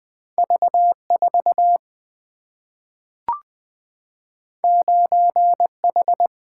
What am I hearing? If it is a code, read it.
Morse code, heard as a keyed tone, about 20 wpm, 700 Hz. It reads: V 4 9 H
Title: Single-Letter Single-Number 20wpm